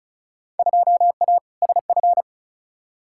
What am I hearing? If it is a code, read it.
Morse code, heard as a keyed tone, about 34 wpm, 700 Hz. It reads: 2A SF
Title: Field Day Exchanges - 35wpm